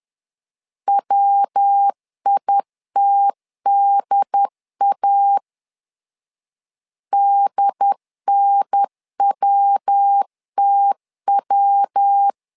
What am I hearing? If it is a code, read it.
Morse code, heard as a keyed tone, about 10 wpm, 800 Hz. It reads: WITDA DNWTW
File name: LEZIONI / 3 MWRO / 3 MW 10wpm 1959